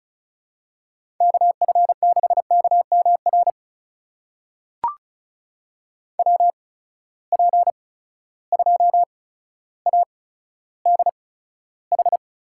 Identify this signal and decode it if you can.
Morse code, heard as a keyed tone, about 36 wpm, spaced as for 15 wpm, 700 Hz. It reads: KF6KMR WP2ADH
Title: USA Call Signs - 6 Characters - 8x Character Spacing - 35wpm